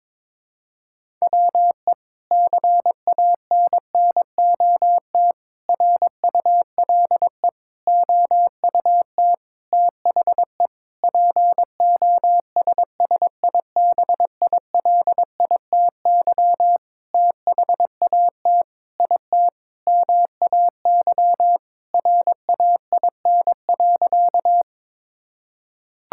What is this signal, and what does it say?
Morse code, heard as a keyed tone, about 21 wpm, 700 Hz. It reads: WE CANNOT RULE OUT THE POSSIBILITY THAT IT MAY RAIN.
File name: practice / 22wpm / Sentences with 8-Word Phrases - 22wpm